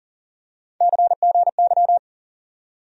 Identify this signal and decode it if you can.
Morse code, heard as a keyed tone, about 40 wpm, 700 Hz. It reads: CGY
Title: USA Call Sign Postfixes - 3 Letters 40wpm